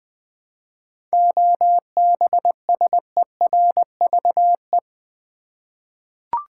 Morse code, heard as a keyed tone, about 20 wpm, 700 Hz. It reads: OBSERVE
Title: 7 Letter Words - First 3 Letters Given - 20wpm